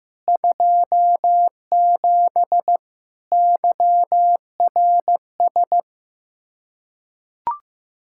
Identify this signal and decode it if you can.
Morse code, heard as a keyed tone, about 16 wpm, 700 Hz. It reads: 27 YRS
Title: Practice QSO - Exchange 3 - More Info - 15wpm